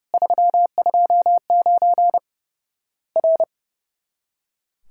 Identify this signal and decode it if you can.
Morse code, heard as a keyed tone, about 30 wpm, 700 Hz. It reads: 329 R
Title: Worked All Europe DX Contest - QTC Exchange - 30wpm